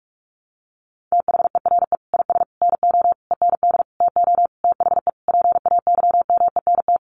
Morse code, heard as a keyed tone, about 44 wpm, 700 Hz. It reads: THERE IS NO END TO THE PAYMENT
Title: Sentences from Top 1200 Words 45wpm